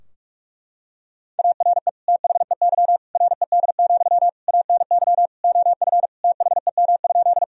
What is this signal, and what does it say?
Morse code, heard as a keyed tone, about 46 wpm, 700 Hz. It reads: ARE THEY RED, ANY OF THEM?
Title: Sentences from Top 1000 Words - Review Part 2 - 45wpm